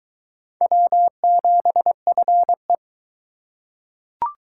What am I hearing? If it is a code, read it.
Morse code, heard as a keyed tone, about 21 wpm, 700 Hz. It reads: W7FE